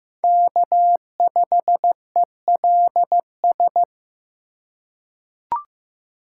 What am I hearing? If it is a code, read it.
Morse code, heard as a keyed tone, about 15 wpm, 700 Hz. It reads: K5ELS
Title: USA Call Signs - 5 Characters - 2x Character Spacing - 15wpm